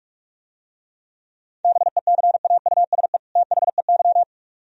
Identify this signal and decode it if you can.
Morse code, heard as a keyed tone, about 43 wpm, 700 Hz. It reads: BECAUSE THEY